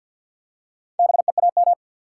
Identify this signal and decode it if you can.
Morse code, heard as a keyed tone, about 48 wpm, 700 Hz. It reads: BEAM
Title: Top 90 QSO Elements 50wpm